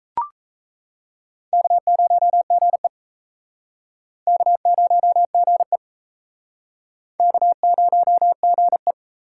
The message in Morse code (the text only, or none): K0GE K0GE K0GE